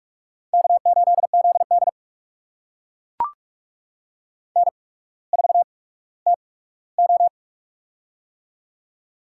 Morse code, heard as a keyed tone, about 44 wpm, spaced as for 19 wpm, 690 Hz. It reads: K8ZD N4TO
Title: USA Call Signs - 4 Characters - 8x Character Spacing - 45wpm